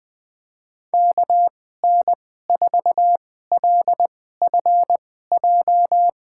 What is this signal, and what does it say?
Morse code, heard as a keyed tone, about 20 wpm, spaced as for 14 wpm, 700 Hz. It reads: KN4LFJ
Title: USA Call Signs - 6 Characters - 2x Character Spacing - 20wpm